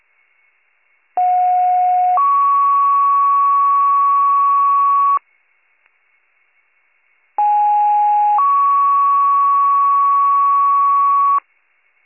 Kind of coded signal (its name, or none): two-tone page